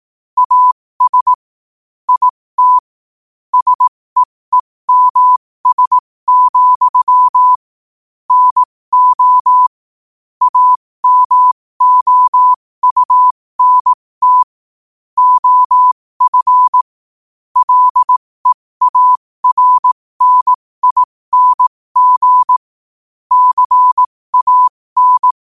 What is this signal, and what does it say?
Morse code, heard as a keyed tone, about 18 wpm, spaced as for 15 wpm, 1000 Hz. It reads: AS IT SEEMS, NO AMOUNT OF LEARNING CAN
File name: qotdfiles / 15wpm